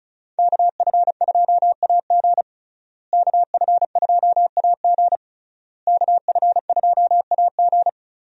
Morse code, heard as a keyed tone, about 36 wpm, 700 Hz. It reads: KF2AG KF2AG KF2AG
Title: USA Call Signs - 5 Characters - Repeated 3x - 35wpm